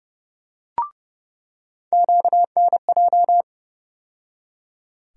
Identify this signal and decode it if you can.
Morse code, heard as a keyed tone, about 30 wpm, 700 Hz. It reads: QNJ
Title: USA Call Sign Postfixes - 3 Letters 30wpm